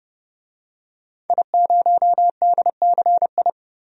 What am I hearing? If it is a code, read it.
Morse code, heard as a keyed tone, about 32 wpm, 700 Hz. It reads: I0DCI
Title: DXCC-100 Call Signs - 5 Characters - 30wpm